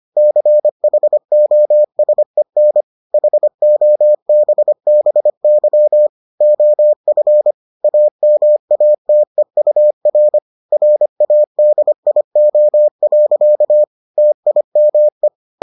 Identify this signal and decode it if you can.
Morse code, heard as a keyed tone, about 25 wpm, 600 Hz. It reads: CHOSEN HOBBY OF AMATEUR RADIO. TIME